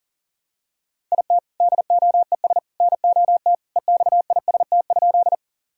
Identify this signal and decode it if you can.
Morse code, heard as a keyed tone, about 40 wpm, 700 Hz. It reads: IT DOES NOT EXIST?
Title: Sentences from Top 1000 Words - Review Part 5 - 40wpm